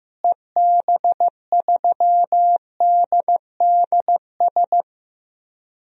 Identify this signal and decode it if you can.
Morse code, heard as a keyed tone, about 15 wpm, 700 Hz. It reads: EB3DDS
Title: DXCC-100 Call Signs - 6 Characters - Repeated 2x - 15wpm